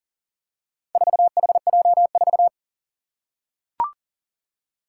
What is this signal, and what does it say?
Morse code, heard as a keyed tone, about 40 wpm, 700 Hz. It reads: 4HJ4
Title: Four Challenging Letter-Number Combinations 40wpm